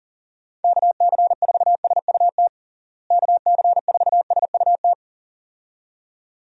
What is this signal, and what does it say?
Morse code, heard as a keyed tone, about 40 wpm, 700 Hz. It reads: KC4SUT KC4SUT
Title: USA Call Signs - 6 Characters - Repeated 2x - 40wpm